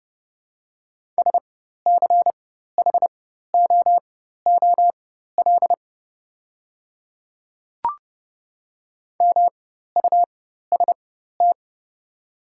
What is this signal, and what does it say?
Morse code, heard as a keyed tone, about 30 wpm, spaced as for 14 wpm, 700 Hz. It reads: SCHOOL MUST
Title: 4x Character Spacing - Top 300 Words - 30wpm